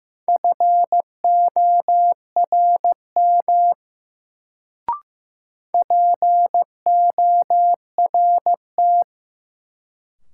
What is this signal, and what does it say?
Morse code, heard as a keyed tone, about 16 wpm, 700 Hz. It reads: FORM PORT
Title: Sentences from Top 1000 Words - Review Part 1 - New Word Drill - 15wpm